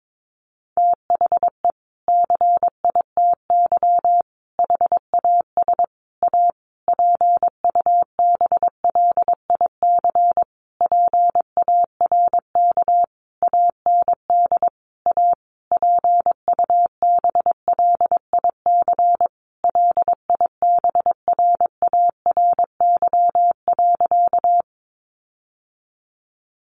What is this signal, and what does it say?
Morse code, heard as a keyed tone, about 22 wpm, 700 Hz. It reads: THE CITY HAS A PUBLIC PARK AND A PUBLIC LIBRARY.
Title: Sentences from Top 1300 Words 22wpm